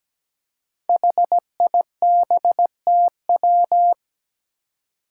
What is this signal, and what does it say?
Morse code, heard as a keyed tone, about 17 wpm, 700 Hz. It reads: HIBTW
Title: Five Challenging Letter-Number Combinations 17wpm